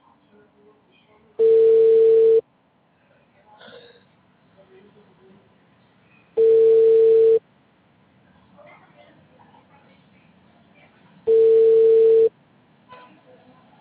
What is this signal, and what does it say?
Ringback tone (China, 450 Hz), with speech.